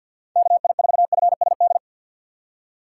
Morse code, heard as a keyed tone, about 49 wpm, 690 Hz. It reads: KI4FSD
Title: USA Call Signs - 6 Characters - 3x Character Spacing - 50wpm